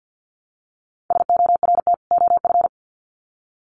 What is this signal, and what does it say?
Morse code, heard as a keyed tone, about 50 wpm, 700 Hz. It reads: SORT OF
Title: Sets of 2 Words 50wpm